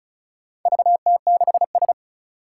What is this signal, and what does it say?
Morse code, heard as a keyed tone, about 36 wpm, 700 Hz. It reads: VT6S
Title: Four Challenging Letter-Number Combinations 35wpm